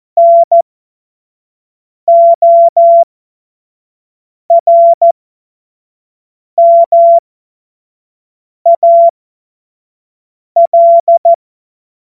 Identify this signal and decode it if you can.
Morse code, heard as a keyed tone, about 14 wpm, spaced as for 5 wpm, 680 Hz. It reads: NORMAL
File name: Morsekurs / Witze / Witz 29 April 2025 25 F70